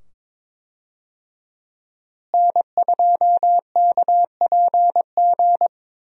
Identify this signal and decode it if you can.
Morse code, heard as a keyed tone, about 22 wpm, 700 Hz. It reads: N2KPG
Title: USA Call Signs - 5 Characters - Repeated 2x - 22wpm